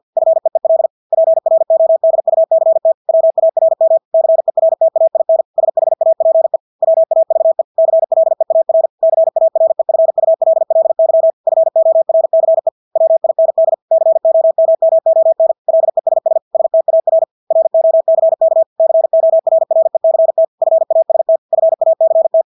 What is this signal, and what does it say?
Morse code, heard as a keyed tone, about 50 wpm, 650 Hz. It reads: FEEL PRODUCT WARM CERTAIN SHAPE WAVE CLEAR CAREFULLY FORCE WIND COMMON LESS STAR ROCK CORRECT FAST FACT
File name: Engelske ord 50 wpm